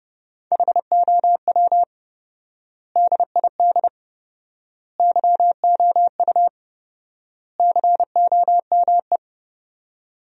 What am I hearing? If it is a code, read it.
Morse code, heard as a keyed tone, about 32 wpm, 700 Hz. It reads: HOW DID YOU COME
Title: 4x Word-Spacing - Sentences from Top 100 Words - 30wpm